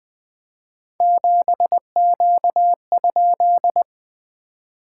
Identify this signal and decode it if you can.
Morse code, heard as a keyed tone, about 20 wpm, 690 Hz. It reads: 7Q?
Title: SOTA QSO Practice 20wpm